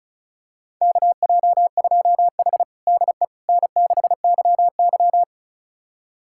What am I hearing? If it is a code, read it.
Morse code, heard as a keyed tone, about 36 wpm, 700 Hz. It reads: KJ2H DE N6YY
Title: USA Callsign DE Callsign - 4 Characters - 35wpm